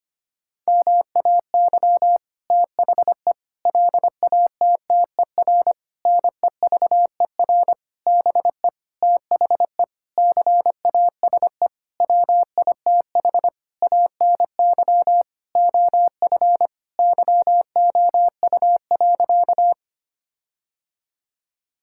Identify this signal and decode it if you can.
Morse code, heard as a keyed tone, about 25 wpm, 700 Hz. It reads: MAY THE LATTER NEVER BE THE CASE WITH ANY OF YOU.
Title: Sentences from Top 1600 Words 25wpm